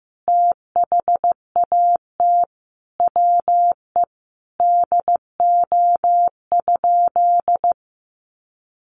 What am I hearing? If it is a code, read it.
Morse code, heard as a keyed tone, about 15 wpm, 700 Hz. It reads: THAT WE DO?
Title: Sentences from Top 1300 Words 15wpm